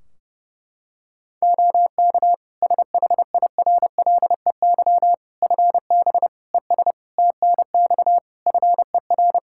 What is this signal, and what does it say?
Morse code, heard as a keyed tone, about 30 wpm, 700 Hz. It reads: OK SHIRLEY FB ES TNX FER
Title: Practice QSO - Exchange 2 - Info - 30wpm